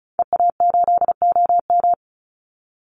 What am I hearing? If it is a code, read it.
Morse code, heard as a keyed tone, about 34 wpm, 700 Hz. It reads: EA8OM
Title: Call Signs Repeated 2x - 35wpm